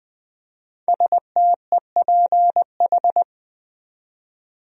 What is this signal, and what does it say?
Morse code, heard as a keyed tone, about 20 wpm, 700 Hz. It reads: STEPH
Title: USA Common Names - 20wpm